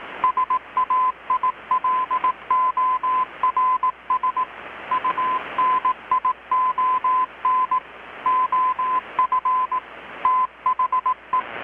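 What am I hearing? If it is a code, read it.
Morse code, heard as a keyed tone, about 18 wpm, 1020 Hz, with radio noise. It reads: SAILORS UNION OF THE
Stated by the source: Morse from KPH's Night of Nights XX maritime radiotelegraphy event, received in Chiapas, Mexico about 3,800 km away on a Kenwood R-600 with a grounded 10 m long-wire antenna.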